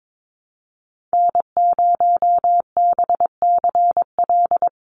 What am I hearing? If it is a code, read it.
Morse code, heard as a keyed tone, about 22 wpm, 700 Hz. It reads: N0BCL
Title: USA Call Signs 22wpm